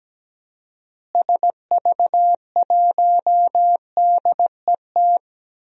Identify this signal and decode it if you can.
Morse code, heard as a keyed tone, about 17 wpm, 700 Hz. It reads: SV1DET
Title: DXCC-100 Call Signs - 6 Characters - Repeated 3x - 17wpm